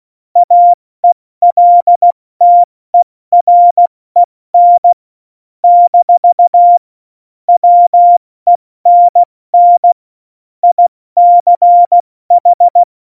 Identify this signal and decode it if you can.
Morse code, heard as a keyed tone, about 16 wpm, 700 Hz. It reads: AELTEREN - WENN ICH
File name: Morsekurs / Witze / Witz 18. Juni 2019, 70 F80